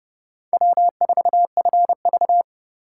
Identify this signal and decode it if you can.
Morse code, heard as a keyed tone, about 29 wpm, 700 Hz. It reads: W4FV